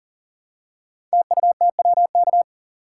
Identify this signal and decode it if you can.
Morse code, heard as a keyed tone, about 39 wpm, 700 Hz. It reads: TUTWK
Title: Code Groups - 5 Letters - 40wpm